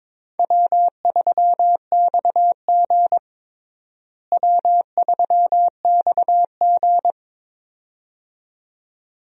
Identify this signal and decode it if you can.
Morse code, heard as a keyed tone, about 22 wpm, 700 Hz. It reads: W3XG W3XG
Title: USA Call Signs - 4 Characters - Repeated 2x - 22wpm